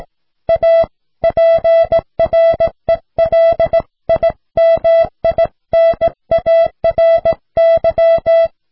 Morse code, heard as a keyed tone, about 17 wpm, 660 Hz. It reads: A PRELIMINARY